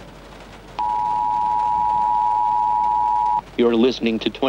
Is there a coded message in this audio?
Yes: Emergency Alert System (SAME) tones (the 853 and 960 Hz attention tone) from 0.8 to 3.5 s, with speech.